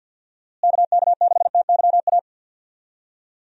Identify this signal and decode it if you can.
Morse code, heard as a keyed tone, about 53 wpm, 700 Hz. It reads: KK6TYA
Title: USA Call Signs - 6 Characters - Repeated 2x - 50wpm